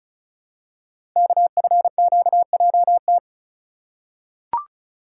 Morse code, heard as a keyed tone, about 36 wpm, 700 Hz. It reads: KFQJT
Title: Code Groups - 5 Letters - 35wpm